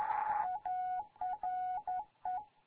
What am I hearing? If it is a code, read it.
Morse code, heard as a keyed tone, about 11 wpm, 740 Hz. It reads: ARE
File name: Morze-HA2MN